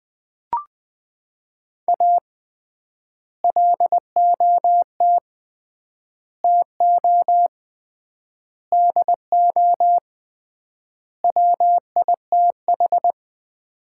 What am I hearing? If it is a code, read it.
Morse code, heard as a keyed tone, about 20 wpm, 700 Hz. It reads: A LOT TO DO WITH
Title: 3x Word-Spacing - Sets of 5 Words - 20wpm